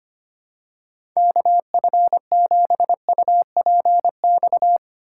Morse code, heard as a keyed tone, about 25 wpm, 700 Hz. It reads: KF7UPX
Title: USA Call Signs - 3x Character Spacing - 25wpm